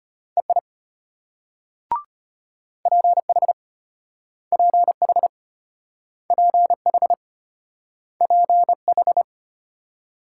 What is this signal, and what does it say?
Morse code, heard as a keyed tone, about 33 wpm, 700 Hz. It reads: EI PH PH PH PH